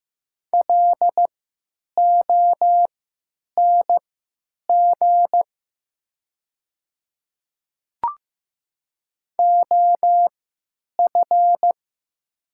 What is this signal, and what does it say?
Morse code, heard as a keyed tone, about 15 wpm, spaced as for 9 wpm, 700 Hz. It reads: LONG OF
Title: 3x Character Spacing - Top 100 Words - 15wpm